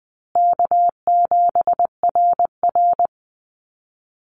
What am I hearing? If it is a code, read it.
Morse code, heard as a keyed tone, about 20 wpm, 700 Hz. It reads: K7RR
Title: Easy Call Signs Repeated 2x - 20wpm